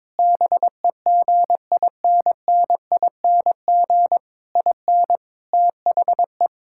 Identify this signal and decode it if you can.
Morse code, heard as a keyed tone, about 22 wpm, 700 Hz. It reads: BEGINNING IN THE
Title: Sentences from Top 1000 Words - Review Part 6 - 22wpm